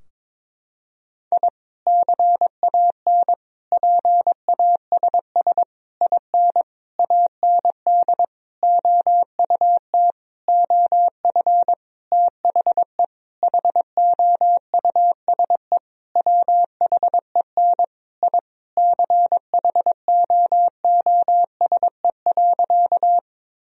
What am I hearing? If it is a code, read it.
Morse code, heard as a keyed tone, about 22 wpm, 700 Hz. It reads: I CAN PASS IN AND OUT OF THE HOUSE WHEN I CHOOSE.
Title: Sentences from Top 1000 Words - Review Part 4 - 22wpm